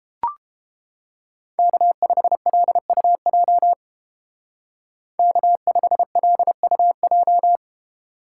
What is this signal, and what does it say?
Morse code, heard as a keyed tone, about 30 wpm, 700 Hz. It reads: K5LUJ K5LUJ